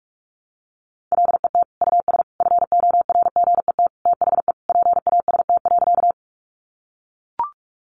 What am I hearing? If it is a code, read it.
Morse code, heard as a keyed tone, about 44 wpm, 700 Hz. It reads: LET US FORGET THE PAST.
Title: Sentences from Top 1200 Words 45wpm